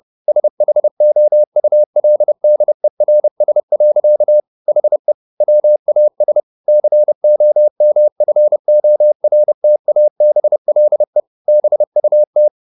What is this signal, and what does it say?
Morse code, heard as a keyed tone, about 30 wpm, 600 Hz. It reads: SHOULDERS. HE WAS COMFORTABLE BUT
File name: the old man and the sea / the old man and the sea 1 30wpm